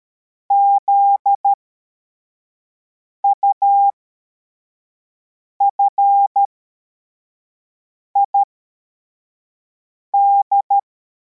Morse code, heard as a keyed tone, about 13 wpm, spaced as for 4 wpm, 800 Hz. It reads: ZUFID
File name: formation / cw / Text64